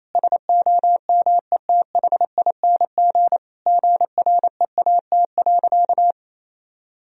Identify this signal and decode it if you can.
Morse code, heard as a keyed tone, about 28 wpm, 700 Hz. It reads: SOMETHING GREAT.
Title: Sentences from Top 1000 Words - Review Part 2 - 28wpm